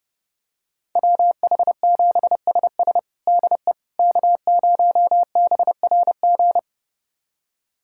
Morse code, heard as a keyed tone, about 30 wpm, 700 Hz. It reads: WH7SS DE K0BRG
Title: USA Callsign DE Callsign - 5 Characters - 30wpm